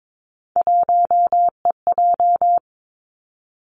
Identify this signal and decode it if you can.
Morse code, heard as a keyed tone, about 22 wpm, 700 Hz. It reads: 1EJ
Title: Three Challenging Letter-Number Combinations 22wpm